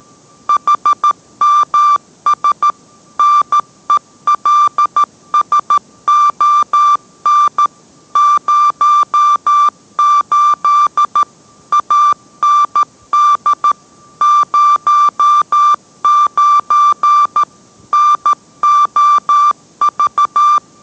Morse code, heard as a keyed tone, about 13 wpm, 1180 Hz. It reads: HMS NELSON 08 AND 09 NOV